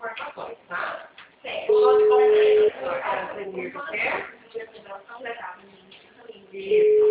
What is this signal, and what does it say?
Ringback tone (Europe, 425 Hz), with speech.